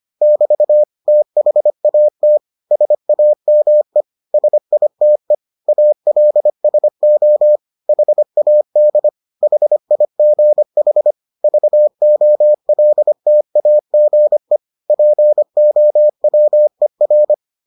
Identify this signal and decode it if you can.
Morse code, heard as a keyed tone, about 25 wpm, 600 Hz. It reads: <BT> THAT SAME SITE ALSO HAD HIGH VOLTAGE POWER